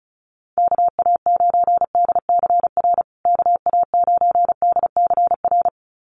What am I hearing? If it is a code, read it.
Morse code, heard as a keyed tone, about 34 wpm, 700 Hz. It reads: KA9DCR KA9DCR
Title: USA Call Signs Repeated 2x 35wpm